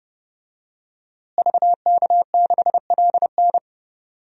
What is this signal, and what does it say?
Morse code, heard as a keyed tone, about 30 wpm, 700 Hz. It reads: VK6LN
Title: DXCC-100 Call Signs - 5 Characters - Repeated 3x - 30wpm